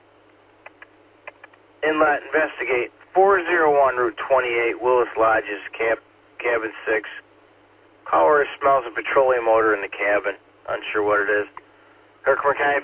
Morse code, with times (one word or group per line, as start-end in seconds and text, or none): none